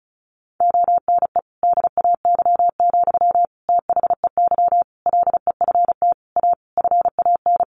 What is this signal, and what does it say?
Morse code, heard as a keyed tone, about 34 wpm, 700 Hz. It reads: ONE DAY, THEY LEFT A FAN
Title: Sentences from Top 1300 Words 35wpm